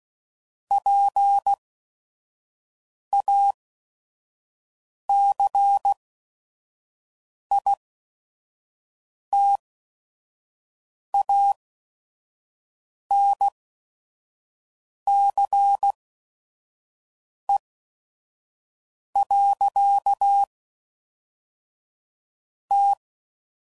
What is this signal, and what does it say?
Morse code, heard as a keyed tone, about 16 wpm, spaced as for 5 wpm, 800 Hz. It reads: PACITANCE. T
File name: CW Code Groups @5WPM / Technical Subjects / XC